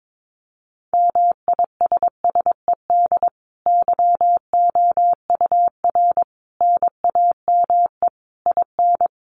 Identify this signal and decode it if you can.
Morse code, heard as a keyed tone, about 22 wpm, 700 Hz. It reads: MISSED YOUR NAME IN
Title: Common QSO Phrases 22wpm